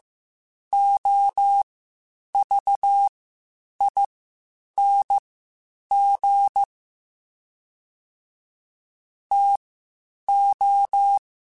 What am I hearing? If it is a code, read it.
Morse code, heard as a keyed tone, about 15 wpm, spaced as for 8 wpm, 800 Hz. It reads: OVING TO
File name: (spaced) / QSOs Spaced-mp3s / WD0BGP 1 @13 sp